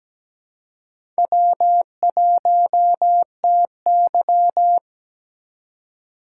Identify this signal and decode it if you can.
Morse code, heard as a keyed tone, about 17 wpm, 700 Hz. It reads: W1TY
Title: USA Call Signs - 4 Characters - 4x Character Spacing - 17wpm